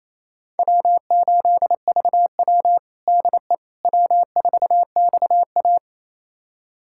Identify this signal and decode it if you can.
Morse code, heard as a keyed tone, about 28 wpm, 700 Hz. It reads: W8VW DE W4XA
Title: USA Callsign DE Callsign - 4 Characters - 28wpm